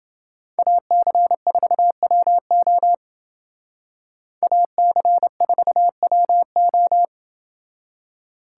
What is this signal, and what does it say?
Morse code, heard as a keyed tone, about 29 wpm, 700 Hz. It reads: AC4WO AC4WO